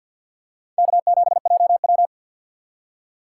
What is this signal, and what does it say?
Morse code, heard as a keyed tone, about 53 wpm, 700 Hz. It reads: K7JW
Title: USA Call Signs - 4 Characters - 6x Character Spacing - 50wpm